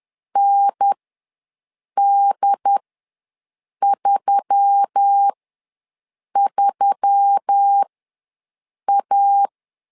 Morse code, heard as a keyed tone, about 10 wpm, spaced as for 6 wpm, 800 Hz. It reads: ND33A
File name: LEZIONI / 4 Q1S3 ws 10wpm 2437